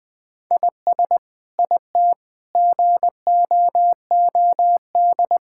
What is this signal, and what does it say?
Morse code, heard as a keyed tone, about 20 wpm, 700 Hz. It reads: IS IT GOOD
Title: Sentences from Top 1000 Words - Review Part 5 - 20wpm